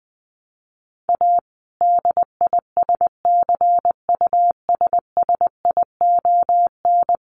Morse code, heard as a keyed tone, about 20 wpm, 700 Hz. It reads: A DISCUSSION
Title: Sentences from Top 1100 Words 20wpm